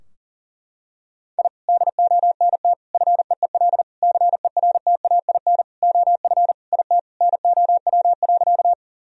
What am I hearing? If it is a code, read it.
Morse code, heard as a keyed tone, about 40 wpm, 700 Hz. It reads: I DONT FEEL CERTAIN OF IT NOW.
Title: Sentences from Top 1000 Words - Review Part 2 - 40wpm